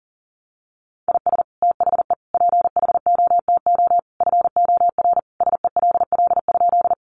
Morse code, heard as a keyed tone, about 40 wpm, 700 Hz. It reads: IS THE PHOTO FOR SELL?
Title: Sentences from Top 1200 Words 40wpm